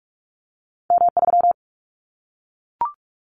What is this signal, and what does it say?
Morse code, heard as a keyed tone, about 46 wpm, 700 Hz. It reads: M3
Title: Callsign Prefixes for Top 25 DXCC - 45wpm